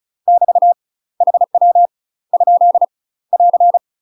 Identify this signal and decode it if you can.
Morse code, heard as a keyed tone, about 36 wpm, 700 Hz. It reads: <BT> HW ? <AR>
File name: UK MASTER QSOs TRAINING/35WPM